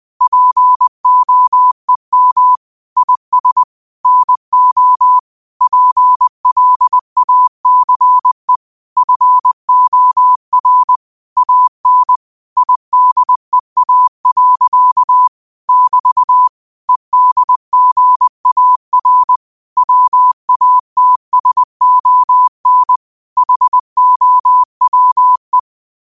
Morse code, heard as a keyed tone, about 20 wpm, 1000 Hz. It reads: POEM IS NO PLACE FOR AN IDEA. <BT> EDGAR WATSON HOWE
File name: qotdfiles / 20wpm